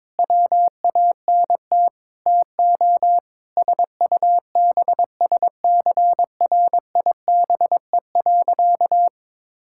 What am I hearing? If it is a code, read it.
Morse code, heard as a keyed tone, about 22 wpm, 700 Hz. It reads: WANT TO SUBSCRIBE.
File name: practice / 22wpm / Sentences with 8-Word Phrases - 22wpm